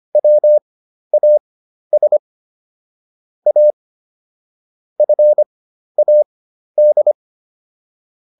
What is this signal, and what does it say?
Morse code, heard as a keyed tone, about 25 wpm, spaced as for 12 wpm, 600 Hz. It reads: WAS A FAD